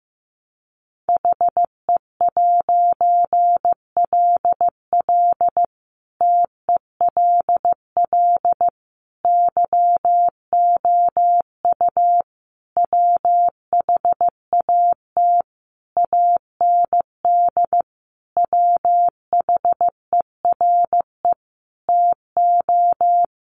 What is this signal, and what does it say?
Morse code, heard as a keyed tone, about 15 wpm, 700 Hz. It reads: HE'LL TELL YOU WHAT AND WHERE TO
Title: A Sound Of Thunder 15wpm